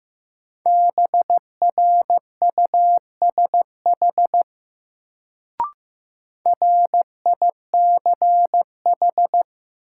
Morse code, heard as a keyed tone, about 15 wpm, 700 Hz. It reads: BRUSH RICH
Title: Sentences from Top 1100 Words - New Word Drill Alt - 15wpm